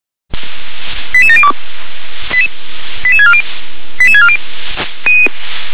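Five-tone selective call (ZVEI), heard as quick runs of five tones, tones of about 80 ms. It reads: 88702 88640 88640
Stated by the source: Five-tone ZVEI selective call received on an SDR.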